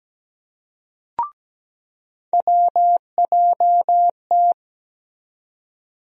Morse code, heard as a keyed tone, about 17 wpm, 700 Hz. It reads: WJT
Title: USA Call Sign Postfixes - 3 Letters 17wpm